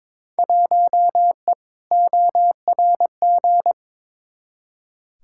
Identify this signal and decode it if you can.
Morse code, heard as a keyed tone, about 22 wpm, 700 Hz. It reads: 1E ORG
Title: Field Day Exchanges - 22wpm